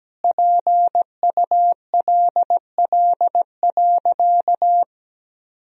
Morse code, heard as a keyed tone, about 17 wpm, 700 Hz. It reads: PULL.
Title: Sentences from Top 1000 Words - Review Part 2 - 17wpm